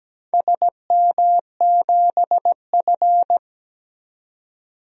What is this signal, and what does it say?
Morse code, heard as a keyed tone, about 17 wpm, 700 Hz. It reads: SM7F
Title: DXCC-100 Call Signs - 4 Characters - 17wpm